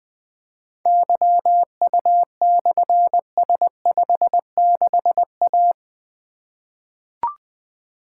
Morse code, heard as a keyed tone, about 20 wpm, 700 Hz. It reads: YU/S56A
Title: Difficult Call Signs 20wpm